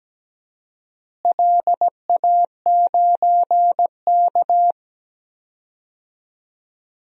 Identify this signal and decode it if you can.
Morse code, heard as a keyed tone, about 17 wpm, 700 Hz. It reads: LA9K
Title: DXCC-100 Call Signs - 4 Characters - Repeated 2x - 17wpm